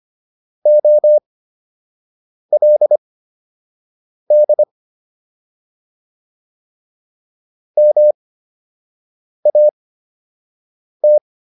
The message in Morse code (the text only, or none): OLD MAT